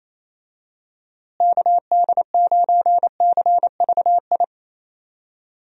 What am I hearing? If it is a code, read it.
Morse code, heard as a keyed tone, about 28 wpm, 700 Hz. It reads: KD9CVI
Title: USA Call Signs - 6 Characters - Repeated 3x - 28wpm